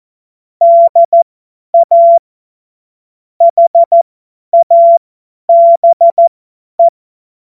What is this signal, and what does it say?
Morse code, heard as a keyed tone, about 14 wpm, spaced as for 10 wpm, 680 Hz. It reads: DA HABE
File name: Morsekurs / Witze / Witz 23. August 2022 50 F70